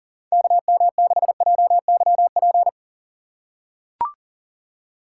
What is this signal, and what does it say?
Morse code, heard as a keyed tone, about 40 wpm, 700 Hz. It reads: KM6JYP
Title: USA Call Signs - 4x Character Spacing - 40wpm